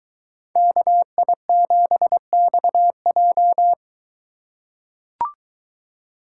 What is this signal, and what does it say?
Morse code, heard as a keyed tone, about 22 wpm, 700 Hz. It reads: KI7XJ